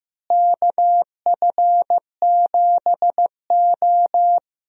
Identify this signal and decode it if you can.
Morse code, heard as a keyed tone, about 15 wpm, 700 Hz. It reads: KF7O
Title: USA Call Signs - 4 Characters - Repeated 3x - 15wpm